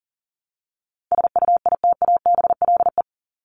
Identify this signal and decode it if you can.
Morse code, heard as a keyed tone, about 40 wpm, 700 Hz. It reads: SUITABLE
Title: Top 1251-1500 Words 40wpm